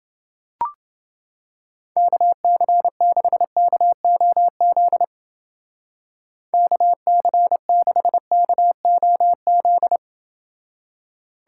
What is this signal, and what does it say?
Morse code, heard as a keyed tone, about 28 wpm, 700 Hz. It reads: KC6KOZ KC6KOZ